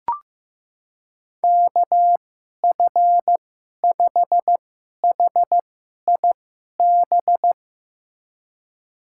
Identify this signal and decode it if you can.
Morse code, heard as a keyed tone, about 15 wpm, spaced as for 11 wpm, 700 Hz. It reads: KF5HIB
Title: USA Call Signs - 2x Character Spacing - 15wpm